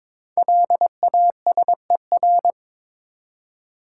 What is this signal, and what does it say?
Morse code, heard as a keyed tone, about 22 wpm, 700 Hz. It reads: LASER
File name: 5 Letter Words - First 3 Letters Given - 22wpm